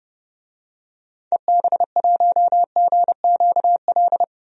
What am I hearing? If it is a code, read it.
Morse code, heard as a keyed tone, about 32 wpm, 700 Hz. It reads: EB1GQL
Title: DXCC-100 Call Signs - 6 Characters - 30wpm